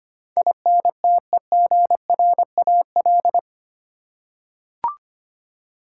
Morse code, heard as a keyed tone, about 25 wpm, 700 Hz. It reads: INTEGRAL
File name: 8 Letter Words - First 3 Letters Given - 25wpm